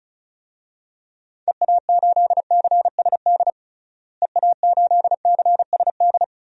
Morse code, heard as a keyed tone, about 36 wpm, 700 Hz. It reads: EA8CSD EA8CSD
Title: DXCC-100 Call Signs - 6 Characters - Repeated 3x - 35wpm